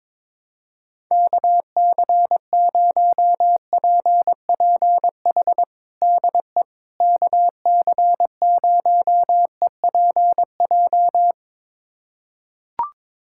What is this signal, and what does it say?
Morse code, heard as a keyed tone, about 22 wpm, 700 Hz. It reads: KC0PPH DE KC0EPJ
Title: USA Callsign DE Callsign 22wpm